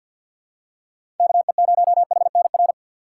Morse code, heard as a keyed tone, about 53 wpm, 700 Hz. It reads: KE0HNR
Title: USA Call Signs - 6 Characters - 8x Character Spacing - 50wpm